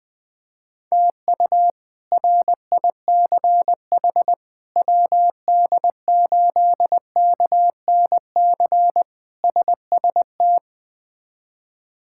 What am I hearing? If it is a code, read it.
Morse code, heard as a keyed tone, about 20 wpm, 700 Hz. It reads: TU RICH WD8KNC SST
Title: K1USN SST QSO Practice - 20wpm